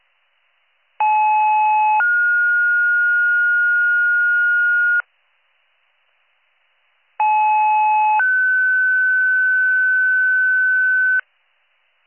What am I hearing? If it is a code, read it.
Two-tone page, heard as a one-second tone, then a three-second tone, about 860 then 1430 Hz, about 860 then 1500 Hz, with noise. It is heard from 1.0 to 11.2 s.